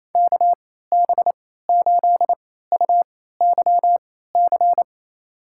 Morse code, heard as a keyed tone, about 27 wpm, spaced as for 16 wpm, 700 Hz. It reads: KB8UYC
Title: USA Call Signs - 6 Characters - 3x Character Spacing - 28wpm